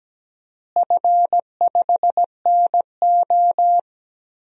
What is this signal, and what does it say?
Morse code, heard as a keyed tone, about 17 wpm, 700 Hz. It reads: F5NO
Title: DXCC-100 Call Signs - 4 Characters - Repeated 2x - 17wpm